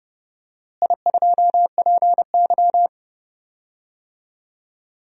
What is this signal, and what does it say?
Morse code, heard as a keyed tone, about 30 wpm, 700 Hz. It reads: I2PY
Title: DXCC-100 Call Signs - 4 Characters - Repeated 3x - 30wpm